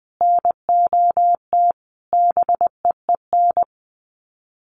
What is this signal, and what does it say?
Morse code, heard as a keyed tone, about 20 wpm, 700 Hz. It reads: NOT BEEN
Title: Sets of 5 Words Encore 20wpm